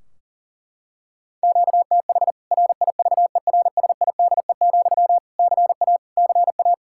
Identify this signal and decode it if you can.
Morse code, heard as a keyed tone, about 40 wpm, 700 Hz. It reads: QTH RIVERSIDE, CA CA
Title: K1USN SST QSO Practice 40wpm